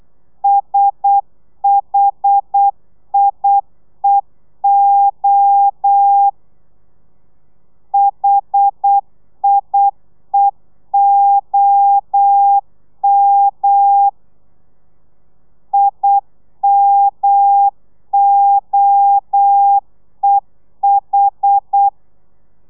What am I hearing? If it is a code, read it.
Morse code, heard as a keyed tone, about 8 wpm, 800 Hz. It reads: SHIEO HIEOM IMOEH
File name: telegrafia / i2tfj / Lez1 35